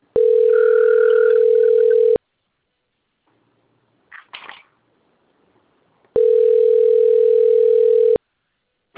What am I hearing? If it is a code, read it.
Ringback tone (North America), 440 + 480 Hz.
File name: out- - - - - .68730